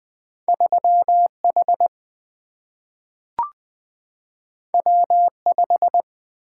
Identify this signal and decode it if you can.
Morse code, heard as a keyed tone, about 20 wpm, 700 Hz. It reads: 3H W5
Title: Single-Letter Single-Number 20wpm